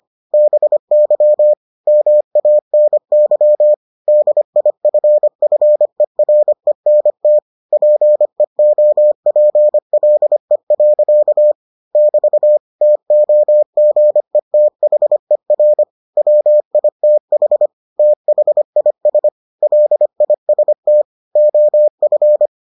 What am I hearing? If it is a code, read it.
Morse code, heard as a keyed tone, about 25 wpm, 600 Hz. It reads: BY MANY DIFFERENT PEOPLE. <BT> TOGETHER WITH THIS LIST OF